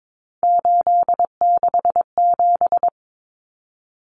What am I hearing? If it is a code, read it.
Morse code, heard as a keyed tone, about 22 wpm, 700 Hz. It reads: 867
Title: Common Cut Numbers - 3-Digit 22wpm